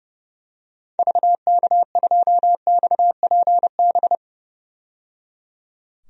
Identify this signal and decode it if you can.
Morse code, heard as a keyed tone, about 30 wpm, 700 Hz. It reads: VK2XPB
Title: DXCC-100 Call Signs - 6 Characters - 30wpm